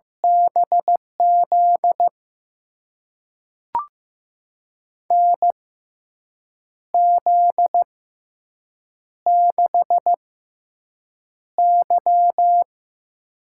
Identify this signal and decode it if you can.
Morse code, heard as a keyed tone, about 15 wpm, spaced as for 6 wpm, 700 Hz. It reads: BZ NZ6Y
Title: USA Call Signs - 6x Character Spacing - 15wpm